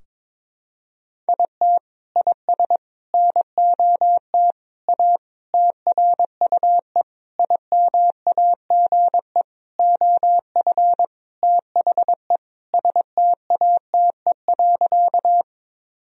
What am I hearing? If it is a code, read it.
Morse code, heard as a keyed tone, about 22 wpm, 700 Hz. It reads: IT IS NOT A TRUE IMAGE OF THE STATE.
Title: Sentences from Top 1000 Words - Review Part 4 - 22wpm